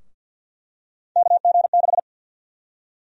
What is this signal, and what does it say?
Morse code, heard as a keyed tone, about 52 wpm, 700 Hz. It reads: KG6
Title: USA Call Sign Prefixes - Repeated 2x - 50wpm